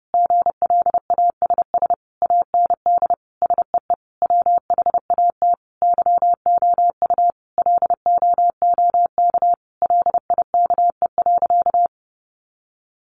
Morse code, heard as a keyed tone, about 30 wpm, 700 Hz. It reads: GLASS AND SEE WHAT YOU LOOK LIKE.
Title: Sentences from Top 1300 Words 30wpm